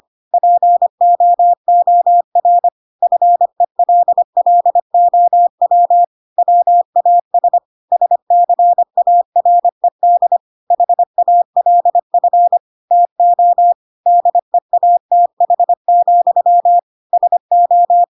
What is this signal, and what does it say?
Morse code, heard as a keyed tone, about 25 wpm, 700 Hz. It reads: POOR FELLOW WAS SCARED HALF TO DEATH, SO
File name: Pinocchio / Pinocchio ch1 25 25 ik1hge